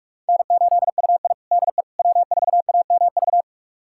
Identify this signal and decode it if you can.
Morse code, heard as a keyed tone, about 46 wpm, 700 Hz. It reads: N8UI DE W4AMV
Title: USA Callsign DE Callsign 45wpm